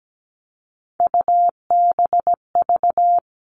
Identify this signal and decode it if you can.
Morse code, heard as a keyed tone, about 17 wpm, 700 Hz. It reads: UBV
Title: All Cut Numbers - 3-Digit 17wpm